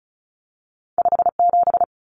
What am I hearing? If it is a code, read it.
Morse code, heard as a keyed tone, about 36 wpm, 700 Hz. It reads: 57
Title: Numbers - 2-Digit 35wpm